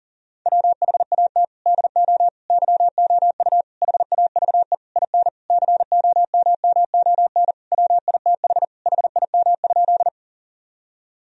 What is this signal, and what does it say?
Morse code, heard as a keyed tone, about 40 wpm, 700 Hz. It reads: WHAT DO YOU HAVE IN COMMON WITH HIM?
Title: Sentences from Top 1000 Words - Review Part 3 - 40wpm